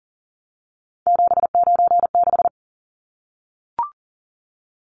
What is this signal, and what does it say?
Morse code, heard as a keyed tone, about 40 wpm, 700 Hz. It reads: 796
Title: Numbers - 3-Digit 40wpm